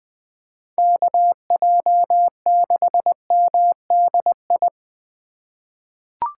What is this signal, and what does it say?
Morse code, heard as a keyed tone, about 20 wpm, 700 Hz. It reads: KJ6MDI
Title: USA Call Signs - 6 Characters - 3x Character Spacing - 20wpm